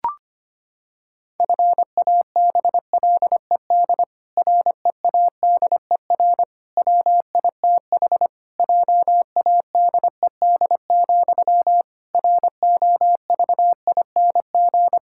Morse code, heard as a keyed tone, about 25 wpm, 700 Hz. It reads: FABLED READER WITH JADED, ROVING
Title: Pangram Challenge v2 - 25wpm